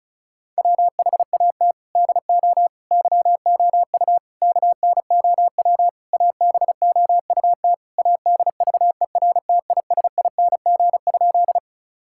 Morse code, heard as a keyed tone, about 36 wpm, 700 Hz. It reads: WHAT DO YOU KNOW ABOUT ADVERTISING?
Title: Sentences from Top 1000 Words - Review Part 6 - 35wpm